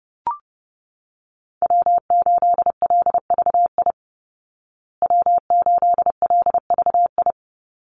Morse code, heard as a keyed tone, about 30 wpm, 700 Hz. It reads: W8LVI W8LVI
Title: Call Signs Repeated 2x - 30wpm